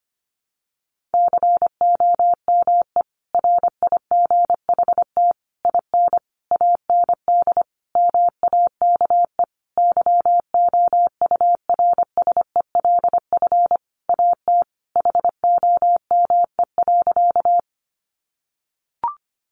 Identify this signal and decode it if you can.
Morse code, heard as a keyed tone, about 25 wpm, 700 Hz. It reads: COME RIGHT IN AND MAKE YOURSELF AT HOME.
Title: Sentences from Top 600 Words 25wpm